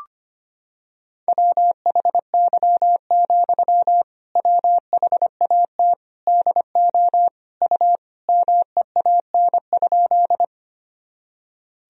Morse code, heard as a keyed tone, about 25 wpm, 700 Hz. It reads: WHY, WHAT DO U MEAN?
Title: Sentences from Top 200 Words and Abbreviations 25wpm